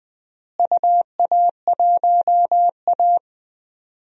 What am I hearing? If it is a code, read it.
Morse code, heard as a keyed tone, about 21 wpm, 700 Hz. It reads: UA1A
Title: DXCC-100 Call Signs - 4 Characters - Repeated 3x - 20wpm